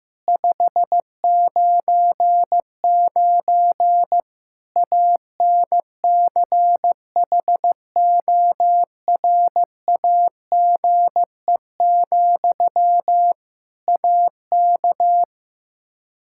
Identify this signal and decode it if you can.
Morse code, heard as a keyed tone, about 15 wpm, 700 Hz. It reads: 599 ANCHORAGE, AK
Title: K1USN SST QSO Practice 15wpm